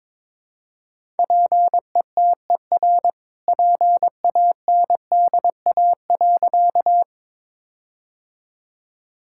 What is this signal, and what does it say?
Morse code, heard as a keyed tone, about 22 wpm, 700 Hz. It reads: PETER PANDA.
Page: morse-code-ninja-practice-22wpm